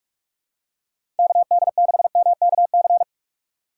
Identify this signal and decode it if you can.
Morse code, heard as a keyed tone, about 46 wpm, 700 Hz. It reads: KD6MKC
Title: USA Call Signs - 6 Characters - 3x Character Spacing - 45wpm